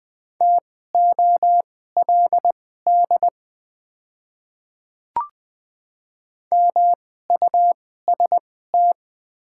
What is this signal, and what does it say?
Morse code, heard as a keyed tone, about 20 wpm, spaced as for 14 wpm, 690 Hz. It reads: TOLD MUST
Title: 2x Character Spacing - Top 500 Words - 20wpm